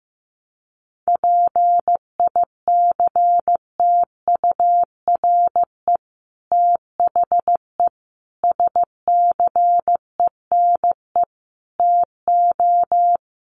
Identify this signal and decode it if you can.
Morse code, heard as a keyed tone, about 15 wpm, 700 Hz. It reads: PICTURE THE SCENE TO
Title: Sentences from Top 1100 Words 15wpm